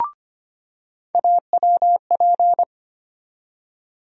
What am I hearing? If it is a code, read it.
Morse code, heard as a keyed tone, about 25 wpm, 700 Hz. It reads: AWP
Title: USA Call Sign Postfixes - 3 Letters 25wpm